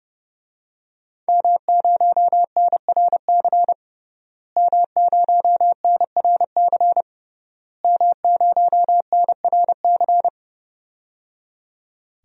Morse code, heard as a keyed tone, about 30 wpm, 700 Hz. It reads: M0NRC M0NRC M0NRC
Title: DXCC-100 Call Signs - 5 Characters - Repeated 3x - 30wpm